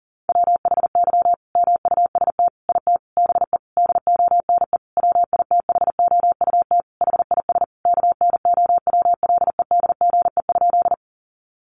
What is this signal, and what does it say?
Morse code, heard as a keyed tone, about 40 wpm, 700 Hz. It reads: WHY MUST IT BE DONE WITHOUT HIS KNOWLEDGE?
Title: Sentences from Top 600 Words 40wpm